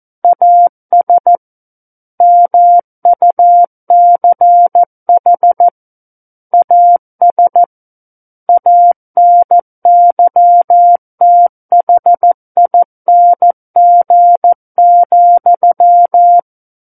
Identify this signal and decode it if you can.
Morse code, heard as a keyed tone, about 14 wpm, 700 Hz. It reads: AS MUCH AS ANYTHING,
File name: Book One / 05-Chapter 5 - 14 WPM